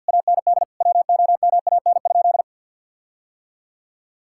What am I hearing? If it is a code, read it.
Morse code, heard as a keyed tone, about 50 wpm, 700 Hz. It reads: AND WOMAN?
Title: Sentences from Top 1600 Words 50wpm